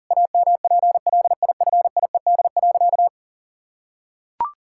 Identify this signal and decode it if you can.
Morse code, heard as a keyed tone, about 40 wpm, 700 Hz. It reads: AMPLIFIED.
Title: Pangram Challenge v2 - 40wpm